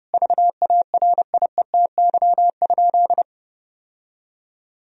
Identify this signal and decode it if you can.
Morse code, heard as a keyed tone, about 30 wpm, 700 Hz. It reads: VARIETY?
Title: Sentences from Top 1000 Words - Review Part 4 - 30wpm